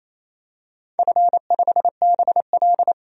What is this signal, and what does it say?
Morse code, heard as a keyed tone, about 28 wpm, 700 Hz. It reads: F5BL
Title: DXCC-100 Call Signs - 4 Characters - 28wpm